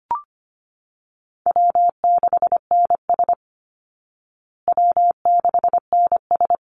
Morse code, heard as a keyed tone, about 25 wpm, 700 Hz. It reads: W6NS W6NS
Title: Call Signs Repeated 3x - 25wpm